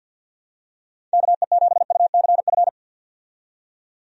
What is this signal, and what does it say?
Morse code, heard as a keyed tone, about 50 wpm, 700 Hz. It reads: KE7UCF
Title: USA Call Signs - 6 Characters - 50wpm